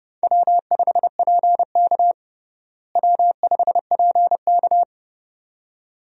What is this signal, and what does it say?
Morse code, heard as a keyed tone, about 30 wpm, 700 Hz. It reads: W5PK W5PK
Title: USA Call Signs - 4 Characters - Repeated 2x - 30wpm